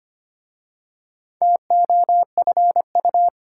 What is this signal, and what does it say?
Morse code, heard as a keyed tone, about 25 wpm, 700 Hz. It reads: TOFU
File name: practice / 25wpm / Difficult Words with Hints - 25wpm